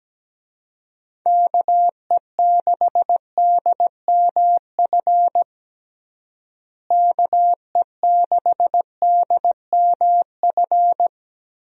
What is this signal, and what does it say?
Morse code, heard as a keyed tone, about 17 wpm, 700 Hz. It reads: KE6DMF KE6DMF
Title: USA Call Signs - 6 Characters - Repeated 3x - 17wpm